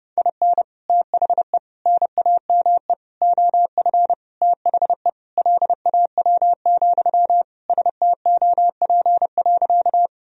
Morse code, heard as a keyed tone, about 30 wpm, 700 Hz. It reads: IN THE NAME OF THE LAW, STOP.
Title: Sentences from Top 1000 Words - Review Part 3 - 30wpm